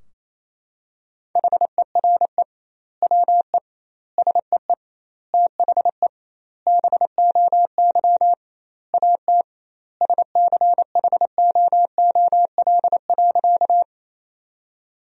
Morse code, heard as a keyed tone, about 28 wpm, 700 Hz. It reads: HERE WE SEE THE BOY AT SCHOOL.
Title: 2x Word-Spacing - Sentences from Top 300 Words - 28wpm